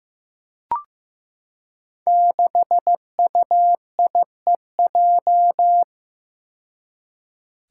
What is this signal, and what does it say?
Morse code, heard as a keyed tone, about 15 wpm, 700 Hz. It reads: 6UIEJ
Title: Five Challenging Letter-Number Combinations 15wpm